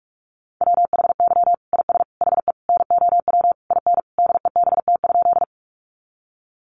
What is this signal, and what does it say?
Morse code, heard as a keyed tone, about 44 wpm, 700 Hz. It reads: WHY IS HE NOW IN DEBT?
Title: Sentences from Top 1300 Words 45wpm